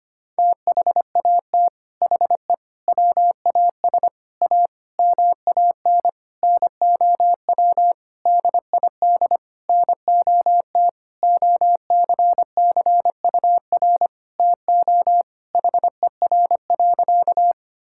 Morse code, heard as a keyed tone, about 25 wpm, 700 Hz. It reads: THAT HE WAS A MAN NOW DID NOT OCCUR TO HER.
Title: Sentences from Top 1000 Words - Review Part 6 - 25wpm